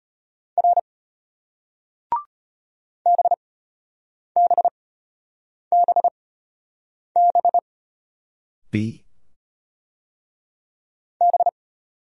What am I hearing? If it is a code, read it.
Morse code, heard as a keyed tone, about 34 wpm, spaced as for 8 wpm, 700 Hz. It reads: R BBBB B